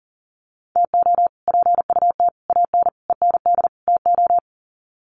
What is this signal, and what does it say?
Morse code, heard as a keyed tone, about 40 wpm, 700 Hz. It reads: TO PUT AN END TO
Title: Sets of 5 Words Encore2 40wpm